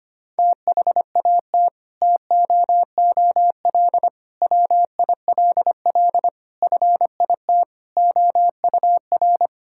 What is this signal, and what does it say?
Morse code, heard as a keyed tone, about 25 wpm, 700 Hz. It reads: THAT TOOL WILL FIT OUR
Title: Sentences from Top 1000 Words - Review Part 6 - 25wpm